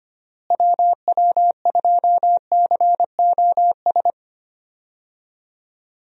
Morse code, heard as a keyed tone, about 25 wpm, 700 Hz. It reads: WW2COS
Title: USA Call Signs - 6 Characters - Repeated 3x - 25wpm